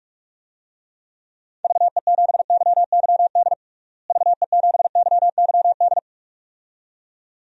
Morse code, heard as a keyed tone, about 41 wpm, 700 Hz. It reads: VE7YYD VE7YYD
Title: DXCC-100 Call Signs - 6 Characters - Repeated 2x - 45wpm